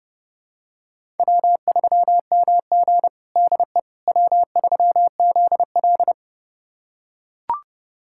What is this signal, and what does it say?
Morse code, heard as a keyed tone, about 30 wpm, 700 Hz. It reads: W3MG DE W3ZL
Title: USA Callsign DE Callsign - 4 Characters - 30wpm